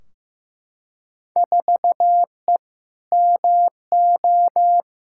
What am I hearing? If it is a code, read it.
Morse code, heard as a keyed tone, about 15 wpm, 700 Hz. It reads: 4E MO
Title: Field Day Exchanges Encore - 15wpm